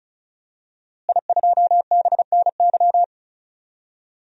Morse code, heard as a keyed tone, about 34 wpm, 700 Hz. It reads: I2BNY
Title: DXCC-100 Call Signs - 5 Characters - Repeated 3x - 35wpm